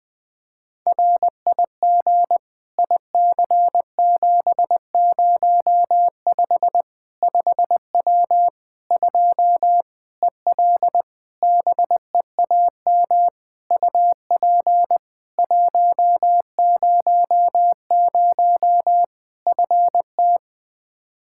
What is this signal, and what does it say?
Morse code, heard as a keyed tone, about 20 wpm, 700 Hz. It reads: RIG IC705 5W 2 EL BEAM UP 100 FT
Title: K1USN SST QSO Practice 20wpm